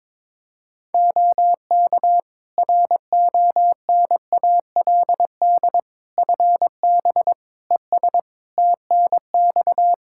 Morse code, heard as a keyed tone, about 22 wpm, 700 Hz. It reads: OK RONALD FB ES TNX
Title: Practice QSO - Exchange 2 - Info - 22wpm